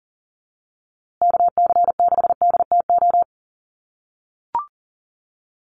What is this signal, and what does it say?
Morse code, heard as a keyed tone, about 40 wpm, 700 Hz. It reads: KC6DTO
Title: USA Call Signs Repeated 3x 40wpm